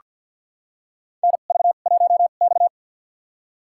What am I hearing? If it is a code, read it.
Morse code, heard as a keyed tone, about 47 wpm, spaced as for 35 wpm, 700 Hz. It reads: NV1X
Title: USA Call Signs - 4 Characters - 2x Character Spacing - 50wpm